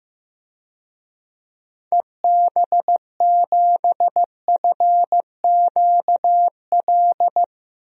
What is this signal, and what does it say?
Morse code, heard as a keyed tone, about 15 wpm, 700 Hz. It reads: EB7FQL
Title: DXCC-100 Call Signs - 6 Characters - 15wpm